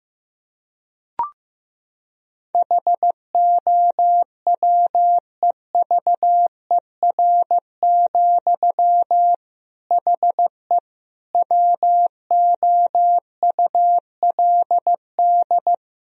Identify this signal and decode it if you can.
Morse code, heard as a keyed tone, about 15 wpm, 700 Hz. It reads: HOWEVER, HE WOULD
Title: Sentences from Top 1000 Words - Review Part 2 - 15wpm